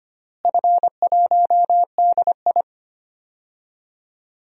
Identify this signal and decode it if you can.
Morse code, heard as a keyed tone, about 25 wpm, 700 Hz. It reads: F1DI
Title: DXCC-100 Call Signs - 4 Characters - Repeated 3x - 25wpm